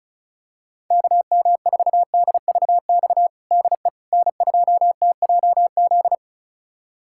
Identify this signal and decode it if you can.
Morse code, heard as a keyed tone, about 36 wpm, 700 Hz. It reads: KM4DVX DE N2TJZ
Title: USA Callsign DE Callsign 35wpm